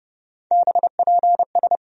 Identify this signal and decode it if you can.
Morse code, heard as a keyed tone, about 30 wpm, 700 Hz. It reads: BPS
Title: USA Call Sign Postfixes - 3 Letters 30wpm